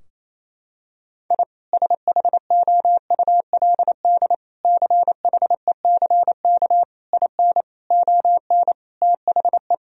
Morse code, heard as a keyed tone, about 28 wpm, 700 Hz. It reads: I SHOULD CHECK IN ON THE
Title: Sentences from Top 1000 Words - Review Part 3 - 28wpm